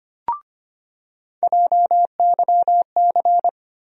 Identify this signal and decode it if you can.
Morse code, heard as a keyed tone, about 25 wpm, 700 Hz. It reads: JYC
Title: USA Call Sign Postfixes - 3 Letters 25wpm